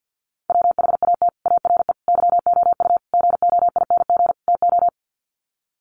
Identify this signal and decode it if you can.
Morse code, heard as a keyed tone, about 48 wpm, 700 Hz. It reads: WHAT ARE YOU GOING TO
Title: Sets of 5 Words 50wpm